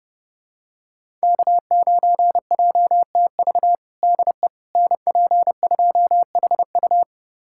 Morse code, heard as a keyed tone, about 30 wpm, 700 Hz. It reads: K9JTV DE NP2HU
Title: USA Callsign DE Callsign - 5 Characters - 30wpm